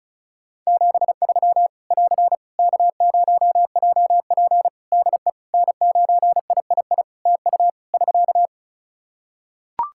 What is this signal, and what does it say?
Morse code, heard as a keyed tone, about 34 wpm, 700 Hz. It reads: 73 <AR> K0JP DE N9III TU <SK>
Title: Practice QSO - Exchange 4 - Ending - 35wpm